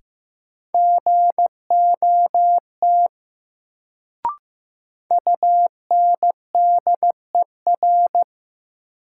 Morse code, heard as a keyed tone, about 15 wpm, 700 Hz. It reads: GOT UNDER
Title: Sentences from Top 300 Words - New Word Drill - 15wpm